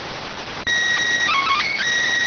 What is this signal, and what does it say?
SSTV (slow-scan television), PD 290.